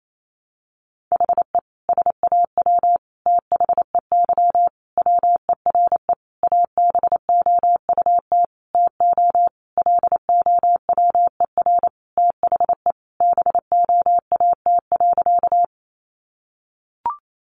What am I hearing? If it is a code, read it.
Morse code, heard as a keyed tone, about 28 wpm, 700 Hz. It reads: HE SAW THEY WERE ABOUT TO LOWER THE BOAT.
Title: Sentences from Top 700 Words 28wpm